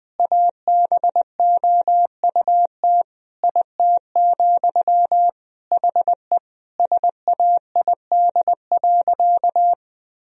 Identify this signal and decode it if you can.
Morse code, heard as a keyed tone, about 20 wpm, 700 Hz. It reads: ABOUT IT, HE SAID.
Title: Sentences from Top 1000 Words - Review Part 3 - 20wpm